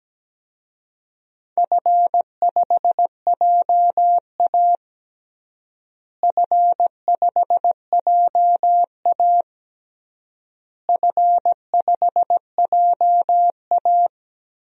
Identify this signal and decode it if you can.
Morse code, heard as a keyed tone, about 17 wpm, 700 Hz. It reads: F5JA F5JA F5JA
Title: DXCC-100 Call Signs - 4 Characters - Repeated 3x - 17wpm